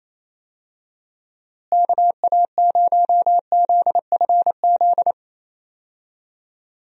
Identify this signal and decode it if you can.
Morse code, heard as a keyed tone, about 28 wpm, 700 Hz. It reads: KA0ZFZ
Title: USA Call Signs - 6 Characters - Repeated 2x - 28wpm